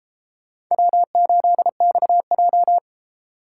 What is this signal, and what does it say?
Morse code, heard as a keyed tone, about 31 wpm, 700 Hz. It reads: W8XJ